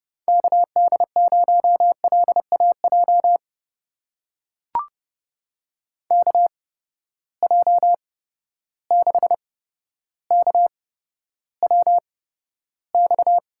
Morse code, heard as a keyed tone, about 32 wpm, spaced as for 13 wpm, 700 Hz. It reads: KD0LAJ KJ6KWX
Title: USA Call Signs - 6 Characters - 8x Character Spacing - 30wpm